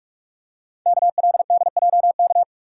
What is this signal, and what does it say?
Morse code, heard as a keyed tone, about 44 wpm, 700 Hz. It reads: KLDJK
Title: Code Groups - 5 Letters - 45wpm